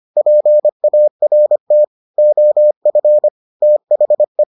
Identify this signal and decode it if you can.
Morse code, heard as a keyed tone, about 25 wpm, 600 Hz. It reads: PART OF THE